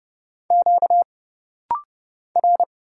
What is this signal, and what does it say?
Morse code, heard as a keyed tone, about 30 wpm, 700 Hz. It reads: Q R
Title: Single Letter-Number Alt - Rapid-Fire 30wpm